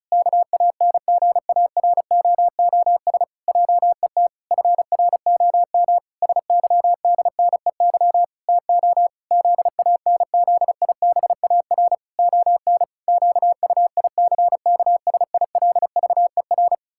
Morse code, heard as a keyed tone, about 36 wpm, 700 Hz. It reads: KANGAROOS JET FROM SYDNEY TO ZANZIBAR ON QUICKSILVER
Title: Pangram Challenge v2 - 35wpm